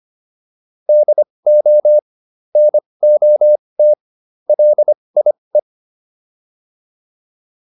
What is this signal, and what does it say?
Morse code, heard as a keyed tone, about 25 wpm, spaced as for 20 wpm, 600 Hz. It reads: DO NOT LIE